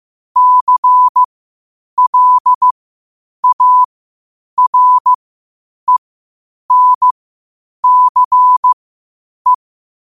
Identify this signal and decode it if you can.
Morse code, heard as a keyed tone, about 15 wpm, spaced as for 8 wpm, 1000 Hz. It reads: CLARENCE
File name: qotdfiles / 8wpm